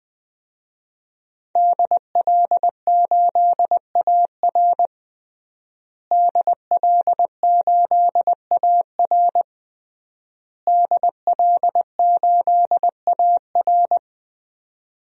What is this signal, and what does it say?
Morse code, heard as a keyed tone, about 20 wpm, 700 Hz. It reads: DL8AR DL8AR DL8AR
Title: DXCC-100 Call Signs - 5 Characters - Repeated 3x - 20wpm